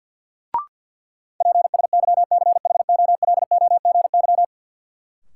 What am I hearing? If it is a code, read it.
Morse code, heard as a keyed tone, about 53 wpm, 700 Hz. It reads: PSYCHOLOGY
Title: Sentences from Top 1400 Words - New Word Drill Alt - 50wpm